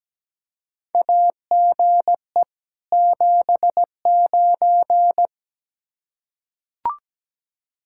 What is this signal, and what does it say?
Morse code, heard as a keyed tone, about 17 wpm, 700 Hz. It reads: AGE 79
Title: K1USN SST QSO Practice 17wpm